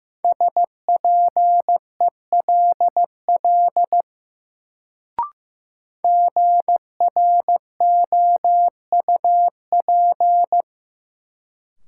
Morse code, heard as a keyed tone, about 15 wpm, 700 Hz. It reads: SPELL GROUP
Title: Sentences from Top 1000 Words - Review Part 1 - New Word Drill - 15wpm